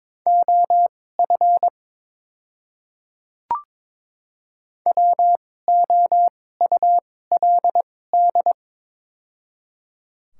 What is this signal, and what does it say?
Morse code, heard as a keyed tone, about 22 wpm, spaced as for 16 wpm, 700 Hz. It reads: OF WOULD
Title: 2x Character Spacing - Top 200 Words - 22wpm